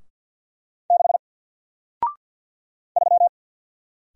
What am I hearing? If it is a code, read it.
Morse code, heard as a keyed tone, about 49 wpm, 700 Hz. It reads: 6 3
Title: Single Letter-Number Alt 50wpm